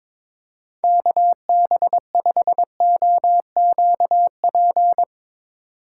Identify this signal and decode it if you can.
Morse code, heard as a keyed tone, about 22 wpm, 700 Hz. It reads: KB5OQP
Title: USA Call Signs - 6 Characters - 3x Character Spacing - 22wpm